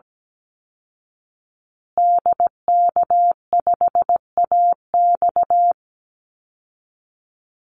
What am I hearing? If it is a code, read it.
Morse code, heard as a keyed tone, about 17 wpm, 700 Hz. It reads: DK5AX
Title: Call Signs Repeated 2x - 17wpm